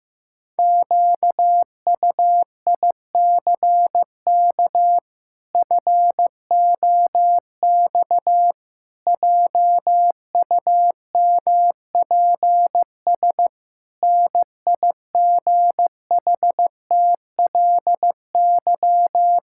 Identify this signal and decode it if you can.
Morse code, heard as a keyed tone, about 15 wpm, 700 Hz. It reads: QUICK FOX JUMPS NIGHTLY
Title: Pangram Challenge v2 - 15wpm